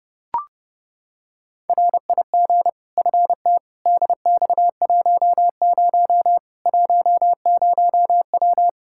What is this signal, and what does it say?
Morse code, heard as a keyed tone, about 32 wpm, 700 Hz. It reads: RIG FT DX10 10W
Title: Practice QSO - Exchange 2 - Info - 30wpm